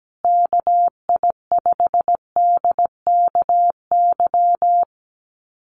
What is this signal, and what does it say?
Morse code, heard as a keyed tone, about 17 wpm, 700 Hz. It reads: KI5DKY
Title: USA Call Signs Repeated 2x 17wpm